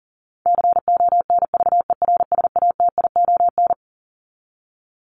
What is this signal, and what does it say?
Morse code, heard as a keyed tone, about 40 wpm, 700 Hz. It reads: CONVERSATION
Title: Top 1751-2000 Words 40wpm